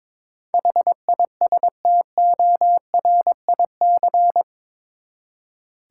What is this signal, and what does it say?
Morse code, heard as a keyed tone, about 22 wpm, 700 Hz. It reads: HISTORIC
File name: practice / 22wpm / Eight Letter Words - 22wpm